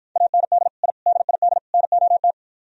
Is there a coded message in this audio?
Yes: Morse code from 0.2 to 2.3 s, heard as a keyed tone, about 52 wpm, 700 Hz. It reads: AND I DID NOT